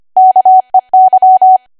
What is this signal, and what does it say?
Morse code, heard as a keyed tone, about 25 wpm, 750 Hz. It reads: KEY
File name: Archive / 25 WPM / 25WPM